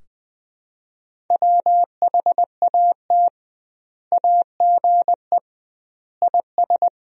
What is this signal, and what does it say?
Morse code, heard as a keyed tone, about 20 wpm, 700 Hz. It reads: WHAT AGE IS
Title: 2x Word-Spacing - Sentences from Top 500 Words - 20wpm